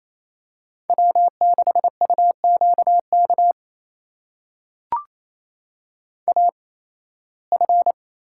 Morse code, heard as a keyed tone, about 27 wpm, spaced as for 18 wpm, 700 Hz. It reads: W6UQK AF
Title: USA Call Signs - 5 Characters - 8x Character Spacing - 28wpm